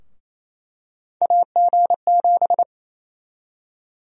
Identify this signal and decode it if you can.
Morse code, heard as a keyed tone, about 28 wpm, 700 Hz. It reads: AG7
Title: USA Call Sign Prefixes - 28wpm